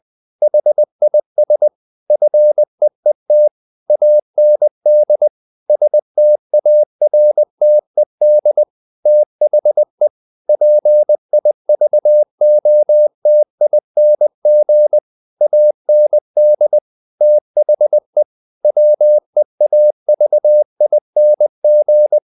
Morse code, heard as a keyed tone, about 20 wpm, 600 Hz. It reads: HIS FEET AND STARTED THE PIVOTING AND THE WEAVING
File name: the old man and the sea part 2 20wpm